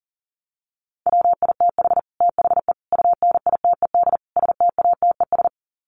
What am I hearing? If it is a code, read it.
Morse code, heard as a keyed tone, about 40 wpm, 700 Hz. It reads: WITH THE UNITED STATES
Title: Sets of 4 Words 40wpm